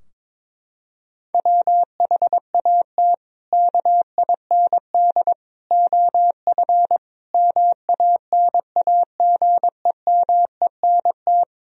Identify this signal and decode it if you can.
Morse code, heard as a keyed tone, about 22 wpm, 700 Hz. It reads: WHAT KIND OF MANAGEMENT
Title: Sentences from Top 1000 Words - Review Part 4 - 22wpm